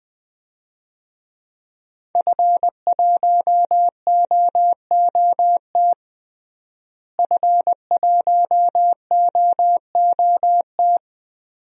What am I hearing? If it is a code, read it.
Morse code, heard as a keyed tone, about 21 wpm, 700 Hz. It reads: F1OOT F1OOT
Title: DXCC-100 Call Signs - 5 Characters - Repeated 2x - 20wpm